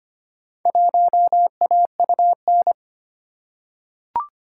Morse code, heard as a keyed tone, about 25 wpm, 690 Hz. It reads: 1AUN
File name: practice / 25wpm / Four Letter-Number Combinations - 25wpm